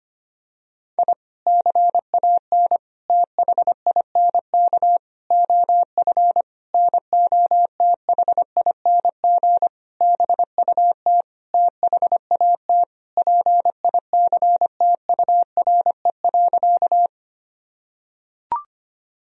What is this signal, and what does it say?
Morse code, heard as a keyed tone, about 25 wpm, 700 Hz. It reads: I CAN THINK OF NOTHING BUT THAT PICTURE.
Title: Sentences from Top 1000 Words - Review Part 1 - 25wpm